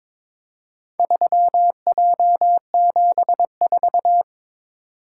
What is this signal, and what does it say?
Morse code, heard as a keyed tone, about 21 wpm, 700 Hz. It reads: 3J74
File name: practice / 22wpm / Four Letter-Number Combinations - 22wpm